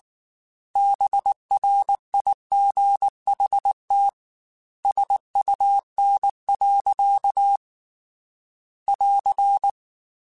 Morse code, heard as a keyed tone, about 19 wpm, 800 Hz. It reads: BRIGHT SUN. <AR>